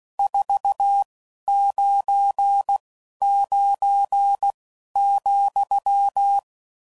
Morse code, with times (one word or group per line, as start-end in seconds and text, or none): 0.2-1.0 4
1.5-2.8 9
3.2-4.5 9
5.0-6.4 ,